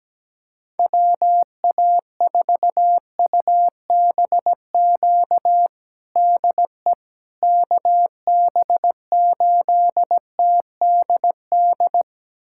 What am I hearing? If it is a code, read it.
Morse code, heard as a keyed tone, about 17 wpm, 700 Hz. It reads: WA4UBQ DE KB8TDD
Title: USA Callsign DE Callsign - 6 Characters - 17wpm